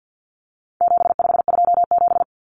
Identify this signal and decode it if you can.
Morse code, heard as a keyed tone, about 50 wpm, 700 Hz. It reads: 7527
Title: Numbers - 4-Digit 50wpm